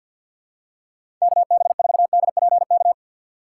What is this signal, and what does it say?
Morse code, heard as a keyed tone, about 53 wpm, 700 Hz. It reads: KB4DPK
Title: USA Call Signs - 4x Character Spacing - 50wpm